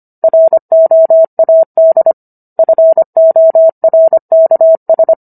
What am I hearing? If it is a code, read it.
Morse code, heard as a keyed tone, about 25 wpm, 650 Hz. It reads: ROAD FORKS